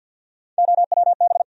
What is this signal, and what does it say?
Morse code, heard as a keyed tone, about 47 wpm, 700 Hz. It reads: OWB
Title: USA Call Sign Postfixes - 3 Letters 50wpm